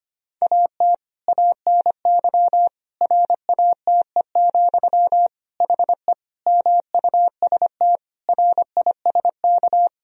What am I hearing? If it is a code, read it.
Morse code, heard as a keyed tone, about 25 wpm, 700 Hz. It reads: AT ANY RATE, HE MUST RISK
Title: Sentences from Top 1000 Words - Review Part 4 - 25wpm